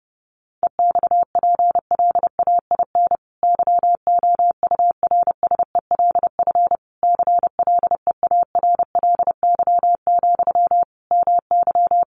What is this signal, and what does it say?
Morse code, heard as a keyed tone, about 30 wpm, 700 Hz. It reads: EXPLAIN YOURSELF CLEARLY, MY
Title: Sentences from Top 1300 Words 30wpm